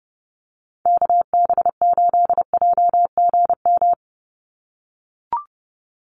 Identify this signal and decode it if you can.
Morse code, heard as a keyed tone, about 30 wpm, 700 Hz. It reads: KB8JGM
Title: USA Call Signs 30wpm